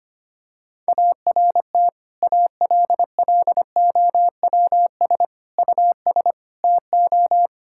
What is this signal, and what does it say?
Morse code, heard as a keyed tone, about 25 wpm, 700 Hz. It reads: ART ALLOWS US TO
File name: practice / 25wpm / Difficult Sentences with Hints - 25wpm